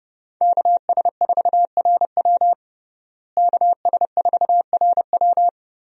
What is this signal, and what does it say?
Morse code, heard as a keyed tone, about 30 wpm, 700 Hz. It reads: KS4RW KS4RW
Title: USA Call Signs - 5 Characters - Repeated 3x - 30wpm